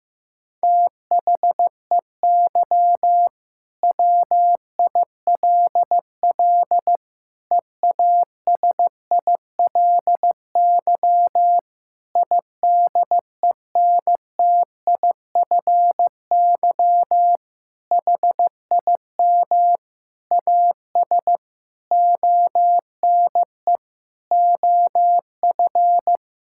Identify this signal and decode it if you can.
Morse code, heard as a keyed tone, about 16 wpm, 700 Hz. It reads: THEY WILL EASILY IDENTIFY HIM AS ONE OF
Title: Sentences from Top 1000 Words - Review Part 6 - 15wpm